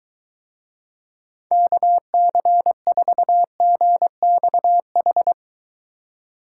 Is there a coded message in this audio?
Yes: Morse code from 1.5 to 5.3 s, heard as a keyed tone, about 23 wpm, 700 Hz. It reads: KC4GXH